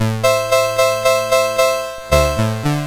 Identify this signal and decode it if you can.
There is no code.